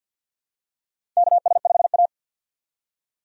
Morse code, heard as a keyed tone, about 51 wpm, 700 Hz. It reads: KS5A
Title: USA Call Signs - 4 Characters - 50wpm